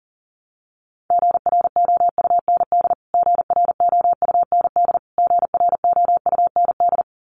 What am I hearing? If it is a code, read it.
Morse code, heard as a keyed tone, about 40 wpm, 700 Hz. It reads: GROUND GROUND GROUND
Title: Top 500 Words Repeated 40wpm